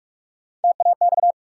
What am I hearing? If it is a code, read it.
Morse code, heard as a keyed tone, about 41 wpm, 700 Hz. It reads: TAX
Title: Sentences from Top 1000 Words - Review Part 5 - New Word Drill - 45wpm